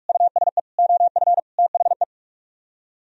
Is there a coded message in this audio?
Yes: Morse code from 0.1 to 2.0 s, heard as a keyed tone, about 58 wpm, 700 Hz. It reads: USE OF THE